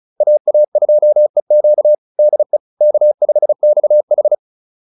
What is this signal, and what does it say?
Morse code, heard as a keyed tone, about 34 wpm, 600 Hz. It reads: AA2EQ DE K5XH